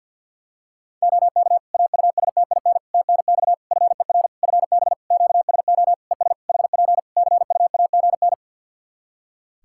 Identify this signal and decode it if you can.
Morse code, heard as a keyed tone, about 48 wpm, 700 Hz. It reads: OK AUSTIN TNX FER FB QSO ES HP CUAGN
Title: Practice QSO - Exchange 4 - Ending - 50wpm